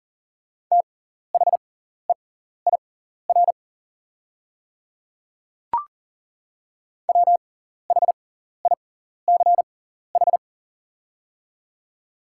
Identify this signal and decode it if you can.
Morse code, heard as a keyed tone, about 40 wpm, spaced as for 14 wpm, 690 Hz. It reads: THEIR WHICH
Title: 6x Character Spacing - Top 300 Words - 40wpm